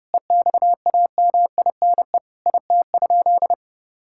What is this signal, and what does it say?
Morse code, heard as a keyed tone, about 30 wpm, 700 Hz. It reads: EXAMINE IT?
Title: Sentences from Top 1600 Words 30wpm